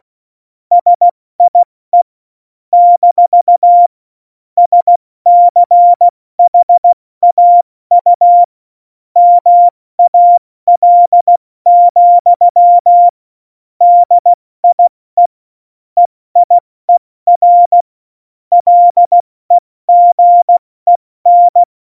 Morse code, heard as a keyed tone, about 16 wpm, 700 Hz. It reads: SIE - SCHAU MAL, DIE EIER LEGEN
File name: Morsekurs / Witze / Witz 01. August 17, Tempo 70 F80